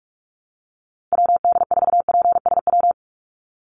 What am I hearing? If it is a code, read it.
Morse code, heard as a keyed tone, about 44 wpm, 700 Hz. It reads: WD4PSW
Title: USA Call Signs 45wpm